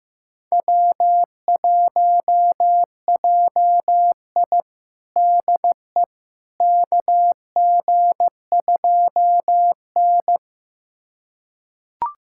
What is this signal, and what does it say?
Morse code, heard as a keyed tone, about 15 wpm, 700 Hz. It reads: W1JI DE KG2N
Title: USA Callsign DE Callsign - 4 Characters - 15wpm